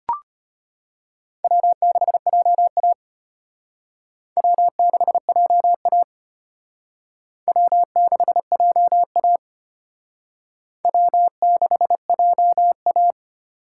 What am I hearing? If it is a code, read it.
Morse code, heard as a keyed tone, about 32 wpm, 700 Hz. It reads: W6JA W6JA W6JA W6JA